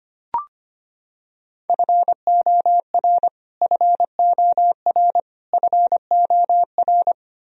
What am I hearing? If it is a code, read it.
Morse code, heard as a keyed tone, about 25 wpm, 700 Hz. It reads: FOR FOR FOR
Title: Top 100 Words Repeated 25wpm